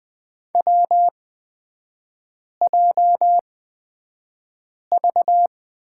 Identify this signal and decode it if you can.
Morse code, heard as a keyed tone, about 20 wpm, spaced as for 5 wpm, 700 Hz. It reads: WJV
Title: 236 Lesson 38e - Letter q - Three Characters 20wpm